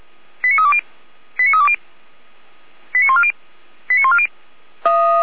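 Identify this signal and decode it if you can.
Five-tone selective call (ZVEI), heard as quick runs of five tones, tones of about 70 ms. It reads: 89320 89320 89140 89140
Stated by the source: ZVEI five-tone selective call received on an SDR, logged by Feuerwehr Orlamünde fire brigade.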